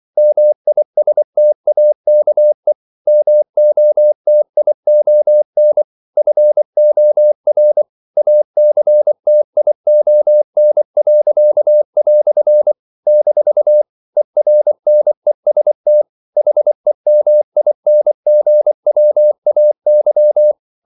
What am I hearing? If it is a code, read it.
Morse code, heard as a keyed tone, about 24 wpm, 600 Hz. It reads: MISTAKE MOTION FOR ACTION." - ERNEST HEMINGWAY
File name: CW / Practice / EnglishQuotations0424